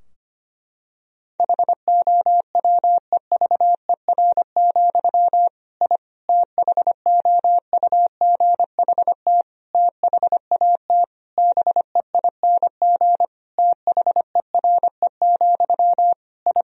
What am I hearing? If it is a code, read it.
Morse code, heard as a keyed tone, about 25 wpm, 700 Hz. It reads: HOWEVER, I THOUGHT THAT BEING THERE, I
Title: Sentences from Top 1000 Words - Review Part 3 - 25wpm